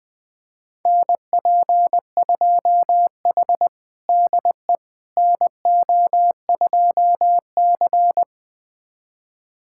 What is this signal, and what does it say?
Morse code, heard as a keyed tone, about 20 wpm, 700 Hz. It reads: NP2H DE NO2C
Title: USA Callsign DE Callsign - 4 Characters - 20wpm